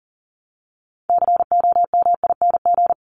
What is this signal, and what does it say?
Morse code, heard as a keyed tone, about 40 wpm, 700 Hz. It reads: COMING
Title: ING Suffix - Top 100 Words with Base Words 40wpm